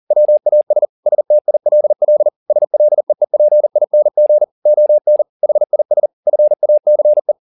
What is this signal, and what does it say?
Morse code, heard as a keyed tone, about 40 wpm, 600 Hz. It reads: WAS STILL SLEEPING ON HIS FACE
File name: the old man and the sea 2 40wpm